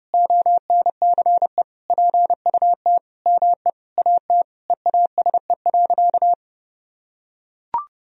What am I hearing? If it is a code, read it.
Morse code, heard as a keyed tone, about 30 wpm, 700 Hz. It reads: ONCE PUT ME AT EASE.
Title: Sentences from Top 1000 Words - Review Part 1 - 30wpm